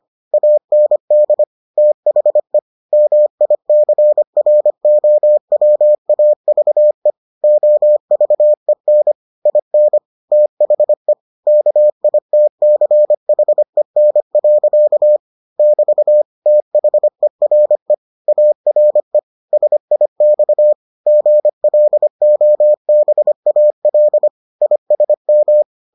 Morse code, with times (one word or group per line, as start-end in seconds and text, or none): 0.3-1.4 AND
1.8-2.6 THE
2.9-7.1 MICROWAVE
7.4-9.1 OVEN
9.5-10.0 IN
10.3-11.1 THE
11.5-15.2 KITCHEN.
15.6-16.2 <BT>
16.5-17.9 THERE
18.3-19.2 ARE
19.5-20.7 SIX
21.1-24.3 GLOBAL
24.6-25.6 ISM